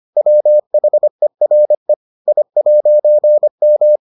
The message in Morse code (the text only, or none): WHERE I'M